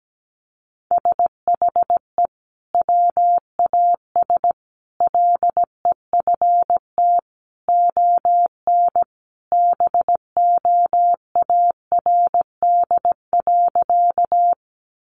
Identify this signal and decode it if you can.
Morse code, heard as a keyed tone, about 17 wpm, 700 Hz. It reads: SHE WAS LEFT ON BOARD.
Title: Sentences from Top 700 Words 17wpm